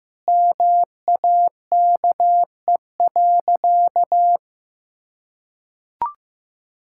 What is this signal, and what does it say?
Morse code, heard as a keyed tone, about 15 wpm, 700 Hz. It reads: MAKE.
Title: Sentences from Top 1000 Words - Review Part 5 - 15wpm